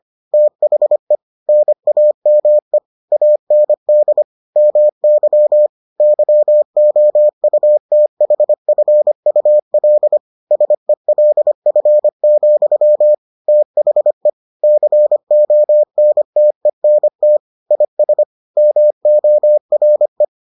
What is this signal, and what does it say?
Morse code, heard as a keyed tone, about 25 wpm, 600 Hz. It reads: THE NAME AND MY YOUTHFUL SELF, THE CONTENT IS MORE